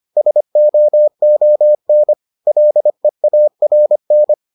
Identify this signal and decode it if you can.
Morse code, heard as a keyed tone, about 25 wpm, 600 Hz. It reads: SOON LEARN